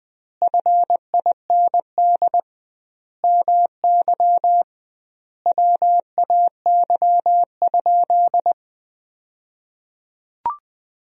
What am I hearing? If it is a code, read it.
Morse code, heard as a keyed tone, about 20 wpm, 700 Hz. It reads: FIND MY WAY?
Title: 2x Word-Spacing - Sentences from Top 200 Words - 20wpm